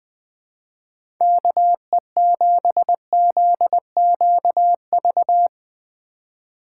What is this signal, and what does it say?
Morse code, heard as a keyed tone, about 20 wpm, 700 Hz. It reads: KE7ZQV
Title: USA Call Signs - 2x Character Spacing - 20wpm